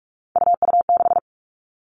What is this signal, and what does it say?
Morse code, heard as a keyed tone, about 44 wpm, 700 Hz. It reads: UU6
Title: All Cut Numbers - 3-Digit 45wpm